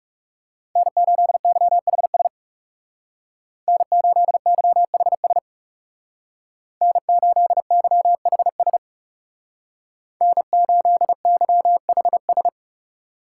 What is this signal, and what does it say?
Morse code, heard as a keyed tone, about 37 wpm, 700 Hz. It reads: N8YHS N8YHS N8YHS N8YHS
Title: Speed Racing - USA 1x3 Call Signs 30wpm